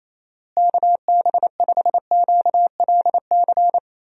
Morse code, heard as a keyed tone, about 27 wpm, 700 Hz. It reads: KB5QLC
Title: USA Call Signs - 6 Characters - Repeated 2x - 28wpm